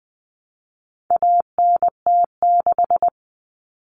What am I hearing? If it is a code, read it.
Morse code, heard as a keyed tone, about 20 wpm, 700 Hz. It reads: ANT6
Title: Common Cut Numbers - 4-Digit 20wpm